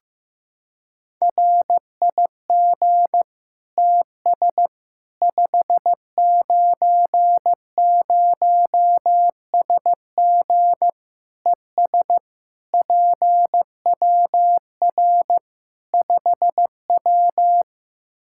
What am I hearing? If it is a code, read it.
Morse code, heard as a keyed tone, about 16 wpm, 700 Hz. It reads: RIG TS 590SG ES PWR 5W
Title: Practice QSO - Exchange 2 - Info - 15wpm